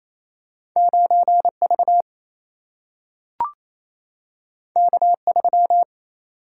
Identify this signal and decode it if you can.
Morse code, heard as a keyed tone, about 28 wpm, 700 Hz. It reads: 9V K3
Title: Single-Letter Single-Number 28wpm